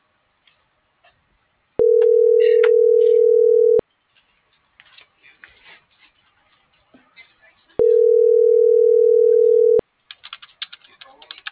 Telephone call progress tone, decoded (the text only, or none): ringback tone